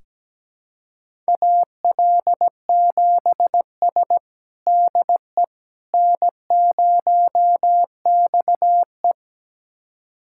Morse code, heard as a keyed tone, about 17 wpm, 700 Hz. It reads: AL7S DE N0XE
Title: USA Callsign DE Callsign - 4 Characters - 17wpm